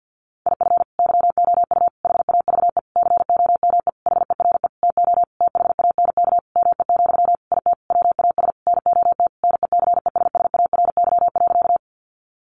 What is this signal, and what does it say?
Morse code, heard as a keyed tone, about 50 wpm, 700 Hz. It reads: IF YOU HAVE COME HERE TO THANK ME, IT WAS NOT NECESSARY.
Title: Sentences from Top 800 Words 50wpm